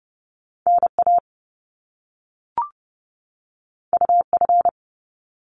Morse code, heard as a keyed tone, about 30 wpm, 700 Hz. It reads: NA UF
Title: Top Two Letter Combinations in Words 30wpm